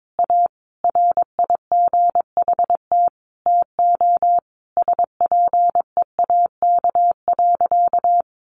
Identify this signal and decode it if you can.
Morse code, heard as a keyed tone, about 22 wpm, 700 Hz. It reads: A RIGHT TO SPEAK.
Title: Sentences from Top 1300 Words 22wpm